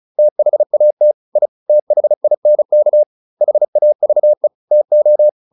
Morse code, heard as a keyed tone, about 34 wpm, 600 Hz. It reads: THAT I THINK HAVE TO
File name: WD402 35